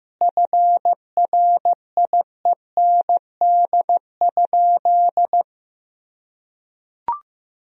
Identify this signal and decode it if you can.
Morse code, heard as a keyed tone, about 15 wpm, 700 Hz. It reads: FRIEND?
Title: Sentences from Top 1000 Words - Review Part 3 - 15wpm